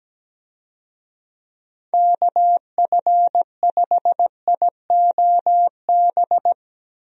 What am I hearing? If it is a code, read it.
Morse code, heard as a keyed tone, about 17 wpm, 700 Hz. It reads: KF5IOB
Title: USA Call Signs - 6 Characters - Repeated 2x - 17wpm